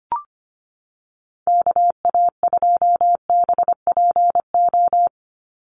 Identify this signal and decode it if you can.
Morse code, heard as a keyed tone, about 25 wpm, 700 Hz. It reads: KA2BPO
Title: USA Call Signs 25wpm